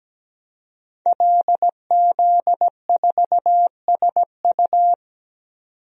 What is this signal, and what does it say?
Morse code, heard as a keyed tone, about 17 wpm, 700 Hz. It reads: LZ4SU
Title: DXCC-100 Call Signs - 5 Characters - 17wpm